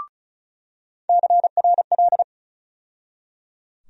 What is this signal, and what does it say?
Morse code, heard as a keyed tone, about 36 wpm, 700 Hz. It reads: CRL
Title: USA Call Sign Postfixes - 3 Letters 35wpm